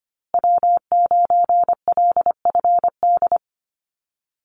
Morse code, heard as a keyed tone, about 25 wpm, 700 Hz. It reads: W9LFD
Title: USA Call Signs Repeated 2x 25wpm